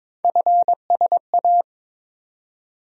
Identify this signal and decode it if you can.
Morse code, heard as a keyed tone, about 22 wpm, 700 Hz. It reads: FSA
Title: USA Call Sign Postfixes - 3 Letters 22wpm